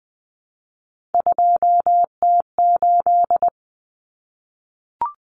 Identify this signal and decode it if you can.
Morse code, heard as a keyed tone, about 20 wpm, 700 Hz. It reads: 2T8
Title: Common Cut Numbers - 3-Digit 20wpm